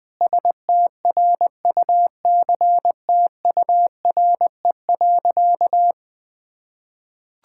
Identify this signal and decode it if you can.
Morse code, heard as a keyed tone, about 20 wpm, 700 Hz. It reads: STRUCTURE.
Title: Sentences from Top 1000 Words - Review Part 6 - 20wpm